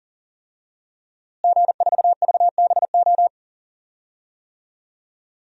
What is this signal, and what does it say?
Morse code, heard as a keyed tone, about 40 wpm, 700 Hz. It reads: G4VBO
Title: DXCC-100 Call Signs - 5 Characters - Repeated 3x - 40wpm